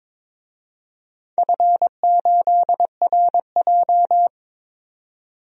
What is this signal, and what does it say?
Morse code, heard as a keyed tone, about 22 wpm, 700 Hz. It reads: F8RJ
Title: DXCC-100 Call Signs - 4 Characters - 22wpm